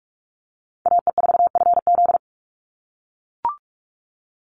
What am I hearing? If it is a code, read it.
Morse code, heard as a keyed tone, about 44 wpm, 700 Hz. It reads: AE4FZ
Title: NA SOTA Chaser Callsigns 45wpm